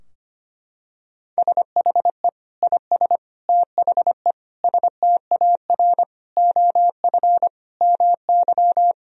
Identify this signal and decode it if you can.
Morse code, heard as a keyed tone, about 25 wpm, 700 Hz. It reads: SHE IS THE STAR OF MY
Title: Sentences from Top 1000 Words - Review Part 2 - 25wpm